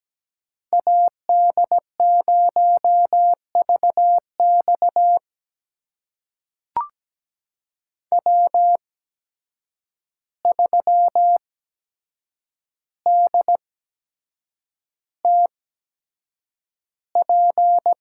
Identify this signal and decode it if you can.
Morse code, heard as a keyed tone, about 17 wpm, spaced as for 7 wpm, 700 Hz. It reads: AD0VX W3DTP
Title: USA Call Signs - 5 Characters - 8x Character Spacing - 17wpm